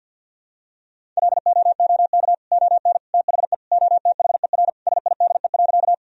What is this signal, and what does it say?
Morse code, heard as a keyed tone, about 48 wpm, 700 Hz. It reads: LOOK ON THE OTHER SIDE.
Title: Sentences from Top 1000 Words - Review Part 1 - 50wpm